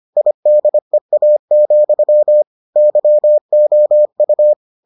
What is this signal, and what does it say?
Morse code, heard as a keyed tone, about 25 wpm, 600 Hz. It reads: IDEA, YOU